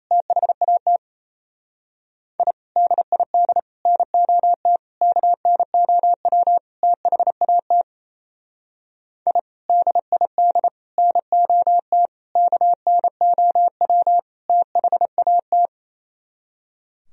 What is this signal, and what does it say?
Morse code, heard as a keyed tone, about 33 wpm, 700 Hz. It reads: THAT I DID NOT KNOW THAT I DID NOT KNOW THAT
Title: Speed-Racing - Sets of 5 Words 28wpm